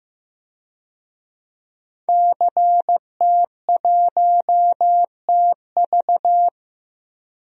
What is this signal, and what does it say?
Morse code, heard as a keyed tone, about 16 wpm, 700 Hz. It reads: CT1TV
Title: DXCC-100 Call Signs - 5 Characters - 15wpm